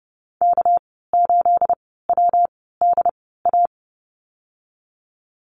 Morse code, heard as a keyed tone, about 30 wpm, spaced as for 17 wpm, 700 Hz. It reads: K8WDA
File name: Lesson 37g - Letter x - Call Signs 30wpm